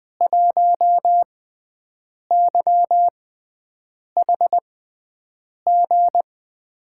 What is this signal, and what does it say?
Morse code, heard as a keyed tone, about 21 wpm, spaced as for 7 wpm, 700 Hz. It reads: 1YHG
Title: USA Call Signs - 5 Characters - 6x Character Spacing - 20wpm